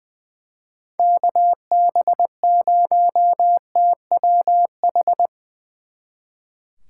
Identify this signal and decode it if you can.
Morse code, heard as a keyed tone, about 22 wpm, 700 Hz. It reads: KB0TWH
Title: Speed Racing - USA Call Signs 20wpm